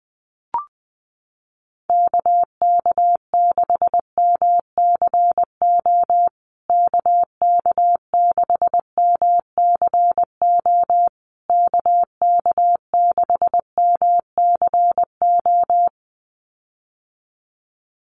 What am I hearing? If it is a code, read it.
Morse code, heard as a keyed tone, about 20 wpm, 700 Hz. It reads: KK6MCO KK6MCO KK6MCO
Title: USA Call Signs Repeated 3x 20wpm